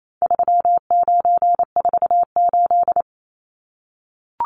Morse code, heard as a keyed tone, about 28 wpm, 700 Hz. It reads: 3948
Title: Numbers - 4-Digit 28wpm